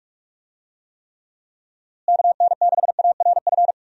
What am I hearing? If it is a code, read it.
Morse code, heard as a keyed tone, about 42 wpm, 700 Hz. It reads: KN6ARF
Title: USA Call Signs - 6 Characters - Repeated 3x - 45wpm